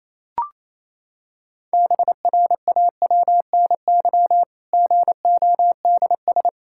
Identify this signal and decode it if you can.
Morse code, heard as a keyed tone, about 28 wpm, 700 Hz. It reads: BRAWNY GODS
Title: Pangram Challenge v2 - 28wpm